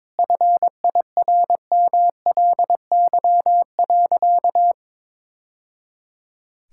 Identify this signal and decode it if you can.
Morse code, heard as a keyed tone, about 22 wpm, 690 Hz. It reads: FIRMLY.
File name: practice / 22wpm / Easy Sentences with Hints - 22wpm